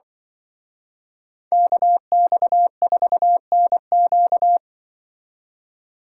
Morse code, heard as a keyed tone, about 24 wpm, 700 Hz. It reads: KX4NQ
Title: Speed Racing - USA 2x2 Call Signs 20wpm